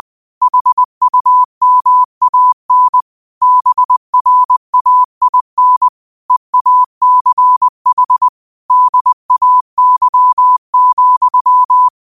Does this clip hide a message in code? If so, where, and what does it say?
Yes: Morse code from 0.4 to 11.9 s, heard as a keyed tone, about 20 wpm, 1000 Hz. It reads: HUMAN BRAIN EACH DAY,